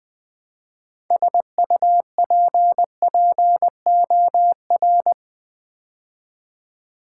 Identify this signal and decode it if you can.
Morse code, heard as a keyed tone, about 20 wpm, 700 Hz. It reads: SUPPOR
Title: Build A Word - 3 Letters At A Time - 9-Letter Words - 20wpm